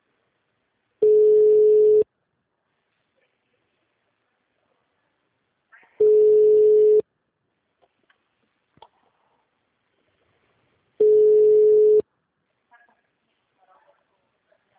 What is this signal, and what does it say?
Ringback tone (Europe, 425 Hz).